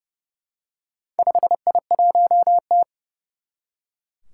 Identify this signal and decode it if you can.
Morse code, heard as a keyed tone, about 29 wpm, 700 Hz. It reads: 5I1T
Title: Four Challenging Letter-Number Combinations 30wpm